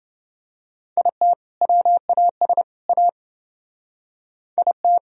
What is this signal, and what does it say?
Morse code, heard as a keyed tone, about 28 wpm, 700 Hz. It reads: IT WAS A IT